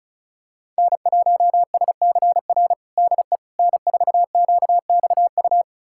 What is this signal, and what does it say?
Morse code, heard as a keyed tone, about 36 wpm, 700 Hz. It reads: N1SCR DE N4QXU
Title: USA Callsign DE Callsign - 5 Characters - 35wpm